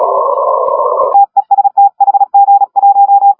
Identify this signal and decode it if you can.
Morse code, heard as a keyed tone, about 37 wpm, 810 Hz. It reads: TEST HG1
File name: HG1R CW SSB MIX